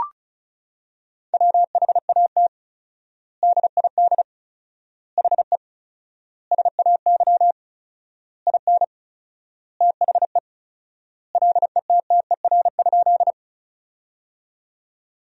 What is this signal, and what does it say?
Morse code, heard as a keyed tone, about 36 wpm, 700 Hz. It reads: WHAT DID HE SAY IN THE LETTER?
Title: 4x Word-Spacing - Sentences from Top 300 Words - 35wpm